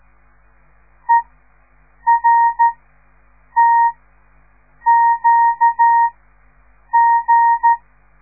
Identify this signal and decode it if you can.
Morse code, heard as a keyed tone, about 13 wpm, spaced as for 7 wpm, 930 Hz. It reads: ERTQG
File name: radiomorningstar / morse / les 6